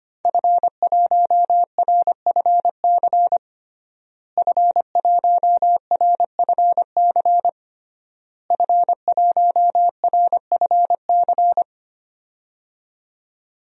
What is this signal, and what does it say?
Morse code, heard as a keyed tone, about 25 wpm, 700 Hz. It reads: F1RFC F1RFC F1RFC
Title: DXCC-100 Call Signs - 5 Characters - Repeated 3x - 25wpm